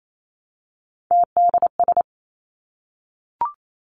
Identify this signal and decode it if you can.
Morse code, heard as a keyed tone, about 28 wpm, 700 Hz. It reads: TDS
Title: Three Challenging Letter-Number Combinations 28wpm